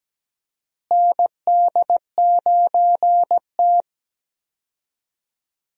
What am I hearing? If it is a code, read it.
Morse code, heard as a keyed tone, about 17 wpm, 700 Hz. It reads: ND9T
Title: USA Call Signs - 4 Characters - Repeated 2x - 17wpm